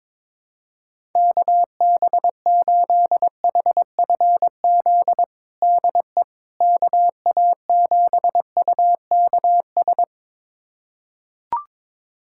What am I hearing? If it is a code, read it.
Morse code, heard as a keyed tone, about 22 wpm, 700 Hz. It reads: KB8HFZ DE KA7UKS
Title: USA Callsign DE Callsign - 6 Characters - 22wpm